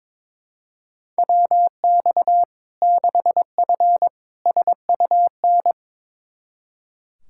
Morse code, heard as a keyed tone, about 22 wpm, 700 Hz. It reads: WX 6F SUN
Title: Practice QSO - Exchange 2 - Info - 22wpm